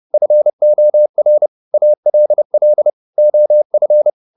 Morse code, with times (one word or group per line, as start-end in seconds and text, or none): 0.1-1.5 FOR
1.7-2.9 ALL
3.2-4.1 OF